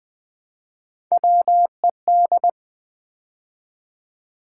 Morse code, heard as a keyed tone, about 20 wpm, 700 Hz. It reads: WED
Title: Build A Word - 3 Letters At A Time - 9-Letter Words - 20wpm